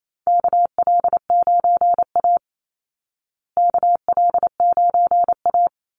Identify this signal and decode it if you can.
Morse code, heard as a keyed tone, about 28 wpm, 700 Hz. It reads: KL9A KL9A
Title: Call Signs Repeated 3x - 28wpm